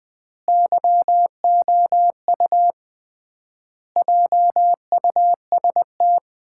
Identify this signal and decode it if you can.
Morse code, heard as a keyed tone, about 20 wpm, 700 Hz. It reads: YOU JUST
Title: 3x Word-Spacing - Sets of 5 Words - 20wpm